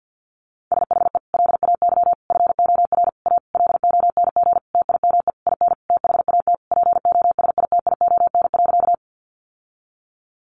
Morse code, heard as a keyed tone, about 50 wpm, 700 Hz. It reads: SHE LAY FOR A LONG TIME IN THAT POSITION.
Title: Sentences from Top 900 Words 50wpm